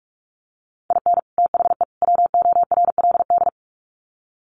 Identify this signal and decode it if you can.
Morse code, heard as a keyed tone, about 44 wpm, 700 Hz. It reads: IN THE WORLD
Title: Sets of 3 Words 45wpm